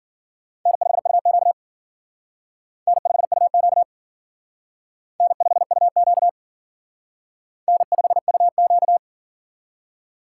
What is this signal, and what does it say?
Morse code, heard as a keyed tone, about 50 wpm, 700 Hz. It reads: N5UQ N5UQ N5UQ N5UQ
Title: Speed Racing - USA 1x2 Call Signs 40wpm